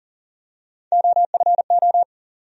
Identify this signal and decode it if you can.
Morse code, heard as a keyed tone, about 38 wpm, 700 Hz. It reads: OFO
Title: USA Call Sign Postfixes - 3 Letters 40wpm